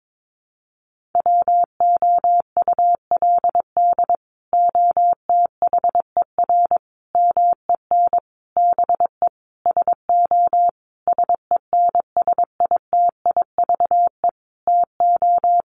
Morse code, heard as a keyed tone, about 22 wpm, 700 Hz. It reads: WOULD OTHER MEN BE SO SENSITIVE TO
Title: Sentences from Top 1400 Words 22wpm